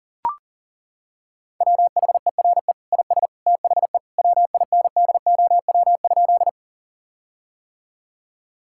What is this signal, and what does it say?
Morse code, heard as a keyed tone, about 37 wpm, 700 Hz. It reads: WHERE IS THE WINDOW?
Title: Sentences from Top 1600 Words 40wpm